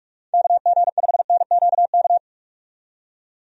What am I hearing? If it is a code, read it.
Morse code, heard as a keyed tone, about 46 wpm, 700 Hz. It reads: KG5NQK
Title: USA Call Signs - 6 Characters - 45wpm